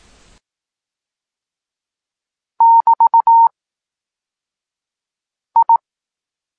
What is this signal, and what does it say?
Morse code, heard as a keyed tone, about 18 wpm, spaced as for 4 wpm, 920 Hz. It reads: <BT>I
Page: Tekst 8 – doskonalenie odbioru poznanych liter